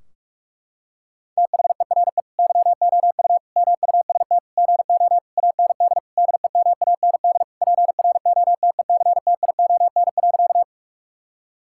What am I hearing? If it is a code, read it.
Morse code, heard as a keyed tone, about 47 wpm, 700 Hz. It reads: THERE YOU MUST GO AND DEMAND PROTECTION.
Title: Sentences from Top 1600 Words 45wpm